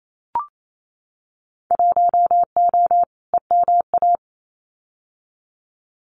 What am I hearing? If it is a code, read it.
Morse code, heard as a keyed tone, about 28 wpm, 700 Hz. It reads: 1O EMA
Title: Winter Field Day Exchanges 28wpm